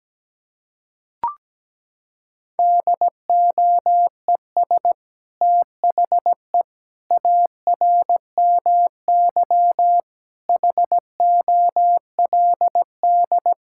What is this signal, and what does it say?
Morse code, heard as a keyed tone, about 17 wpm, 700 Hz. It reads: DOES THE ARMY HOLD
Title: Sentences from Top 1000 Words - Review Part 6 - 17wpm